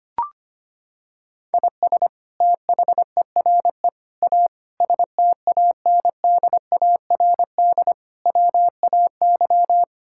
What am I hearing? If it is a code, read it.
Morse code, heard as a keyed tone, about 25 wpm, 700 Hz. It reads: IS THERE A STANDARD WAY
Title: Sentences from Top 1000 Words - Review Part 6 - 25wpm